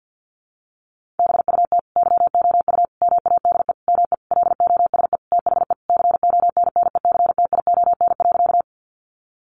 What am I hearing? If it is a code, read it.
Morse code, heard as a keyed tone, about 50 wpm, 700 Hz. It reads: BUT YOU MADE ME LOSE THE CONNECTION.
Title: Sentences from Top 1300 Words 50wpm